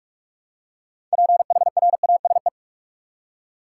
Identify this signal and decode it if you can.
Morse code, heard as a keyed tone, about 46 wpm, 700 Hz. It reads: PHRASE
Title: Six Letter Words - 45wpm